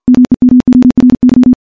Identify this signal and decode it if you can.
Morse code, heard as a keyed tone, about 51 wpm, 250 Hz. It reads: MEMORY
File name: morse / morse-training-50WPM